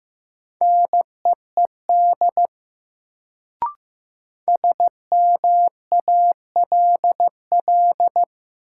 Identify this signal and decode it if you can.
Morse code, heard as a keyed tone, about 15 wpm, 700 Hz. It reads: NEED SMALL
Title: Sets of 2 Words - New Word Drill - 15wpm